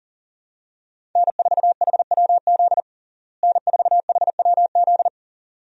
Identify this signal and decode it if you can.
Morse code, heard as a keyed tone, about 37 wpm, 700 Hz. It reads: N4HWZ N4HWZ
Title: USA Call Signs - 5 Characters - Repeated 2x - 40wpm